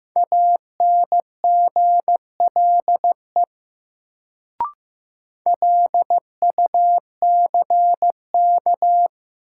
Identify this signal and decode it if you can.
Morse code, heard as a keyed tone, about 15 wpm, 700 Hz. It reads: ANGLE LUCK
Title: Sentences from Top 1500 Words - New Word Drill - 15wpm